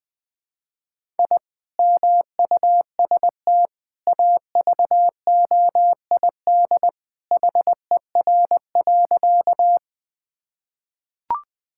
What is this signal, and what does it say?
Morse code, heard as a keyed tone, about 20 wpm, 700 Hz. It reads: I MUST AVOID HER.
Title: Sentences from Top 1000 Words - Review Part 6 - 20wpm